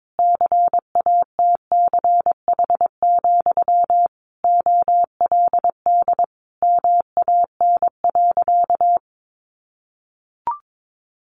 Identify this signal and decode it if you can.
Morse code, heard as a keyed tone, about 22 wpm, 700 Hz. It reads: CATCH, OLD MAN.
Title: Sentences from Top 1500 Words 22wpm